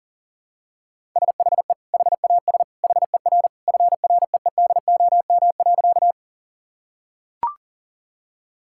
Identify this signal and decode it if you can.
Morse code, heard as a keyed tone, about 40 wpm, 700 Hz. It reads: SHE HAS HER FREEDOM.
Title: Sentences from Top 1000 Words - Review Part 5 - 40wpm